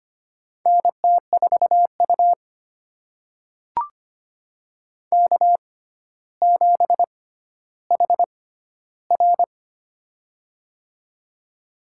Morse code, heard as a keyed tone, about 25 wpm, spaced as for 13 wpm, 700 Hz. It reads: NT4U K7HR
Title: USA Call Signs - 4 Characters - 6x Character Spacing - 25wpm